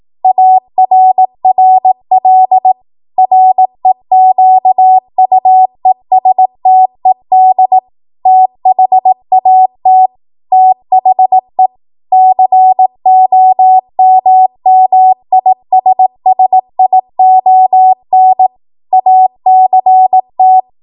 Morse code, heard as a keyed tone, about 18 wpm, 750 Hz. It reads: ARRL REQUESTED THAT THE COMMISSION ACT
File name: Morse / 18WPM